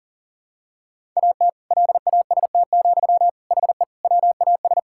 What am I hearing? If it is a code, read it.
Morse code, heard as a keyed tone, about 37 wpm, 700 Hz. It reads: AT LAST, HE WAS
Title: Sentences from Top 1000 Words - Review Part 6 - 40wpm